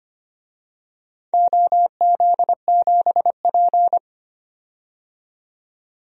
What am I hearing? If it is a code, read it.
Morse code, heard as a keyed tone, about 25 wpm, 700 Hz. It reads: OZ7P
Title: DXCC-100 Call Signs - 4 Characters - 25wpm